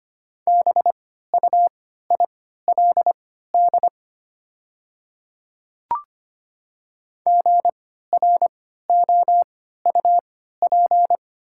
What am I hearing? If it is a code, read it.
Morse code, heard as a keyed tone, about 25 wpm, spaced as for 14 wpm, 700 Hz. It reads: BUILD GROUP
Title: 3x Character Spacing - Top 500 Words - 25wpm